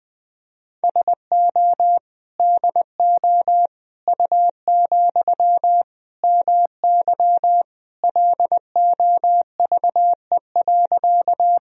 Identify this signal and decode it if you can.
Morse code, heard as a keyed tone, about 20 wpm, 700 Hz. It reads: SO DO U, MY LOVE.
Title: Sentences from Top 400 Words and Abbreviations 20wpm